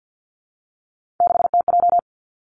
Morse code, heard as a keyed tone, about 50 wpm, 700 Hz. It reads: 6TJ
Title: Three Challenging Letter-Number Combinations 50wpm